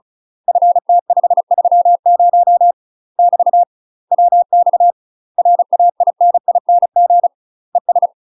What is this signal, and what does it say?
Morse code, heard as a keyed tone, about 35 wpm, 700 Hz. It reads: FT530 <BT> WX RAINING ES
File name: UK MASTER QSOs TRAINING / 35WPM / UK MASTER QSOs 0031